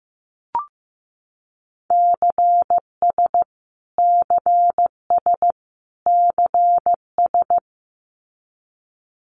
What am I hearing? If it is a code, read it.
Morse code, heard as a keyed tone, about 15 wpm, 700 Hz. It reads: CS CS CS
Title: Abbreviations Repeated 3x - 15wpm